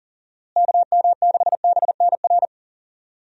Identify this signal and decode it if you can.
Morse code, heard as a keyed tone, about 37 wpm, 700 Hz. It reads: KM6BNR
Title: USA Call Signs - 6 Characters - Repeated 3x - 40wpm